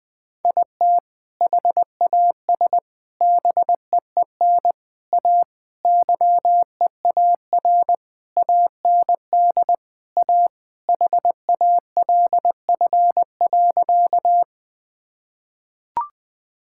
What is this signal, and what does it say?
Morse code, heard as a keyed tone, about 20 wpm, 700 Hz. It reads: IT HAS BEEN A YEAR AND A HALF.
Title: Sentences with 5-Word Phrases - Spoken Beginning - 20wpm